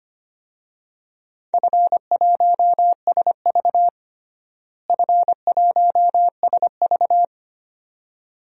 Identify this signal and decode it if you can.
Morse code, heard as a keyed tone, about 25 wpm, 700 Hz. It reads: F1SV F1SV
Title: DXCC-100 Call Signs - 4 Characters - Repeated 2x - 25wpm